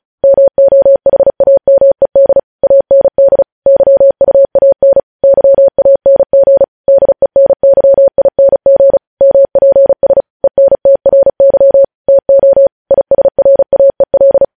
Morse code, heard as a keyed tone, about 36 wpm, 550 Hz. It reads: MOHAMED AND YUAN YANG DENYING MPS ENTRY TO ISRAEL
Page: News Headlines in Morse Code at 35 WPM